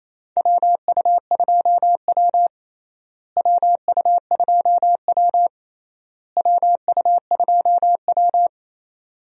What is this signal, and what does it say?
Morse code, heard as a keyed tone, about 27 wpm, 700 Hz. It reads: WU2W WU2W WU2W
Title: USA Call Signs - 4 Characters - Repeated 3x - 28wpm